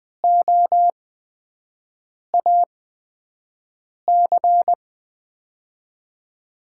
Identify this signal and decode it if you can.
Morse code, heard as a keyed tone, about 20 wpm, spaced as for 5 wpm, 700 Hz. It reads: OAC
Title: USA Call Signs - 5 Characters - 8x Character Spacing - 20wpm